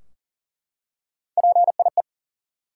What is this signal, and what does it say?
Morse code, heard as a keyed tone, about 40 wpm, 700 Hz. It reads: PIE
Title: Three Letter Words - 40wpm